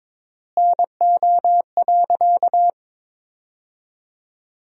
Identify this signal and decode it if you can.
Morse code, heard as a keyed tone, about 21 wpm, 700 Hz. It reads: NO.
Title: 3x Word-Spacing - Sentences from Top 500 Words - 22wpm